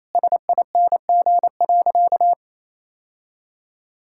Morse code, heard as a keyed tone, about 28 wpm, 700 Hz. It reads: SING.
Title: Sentences from Top 1600 Words 28wpm